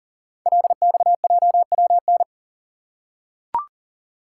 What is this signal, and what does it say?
Morse code, heard as a keyed tone, about 39 wpm, 700 Hz. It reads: LXJWN
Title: Code Groups - 5 Letter:Numbers - 40wpm